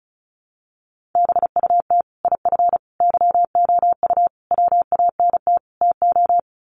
Morse code, heard as a keyed tone, about 34 wpm, 700 Hz. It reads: BUT IF YOU WANT TO
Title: Sets of 5 Words Encore2 35wpm